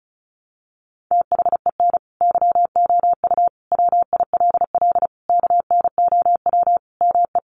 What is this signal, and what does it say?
Morse code, heard as a keyed tone, about 34 wpm, 700 Hz. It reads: THEN YOU WILL KNOW ME
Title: Sentences from Top 400 Words 35wpm